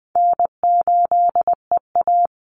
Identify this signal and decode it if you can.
Morse code, heard as a keyed tone, about 20 wpm, 700 Hz. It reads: N8EA
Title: Easy Call Signs Repeated 2x - 20wpm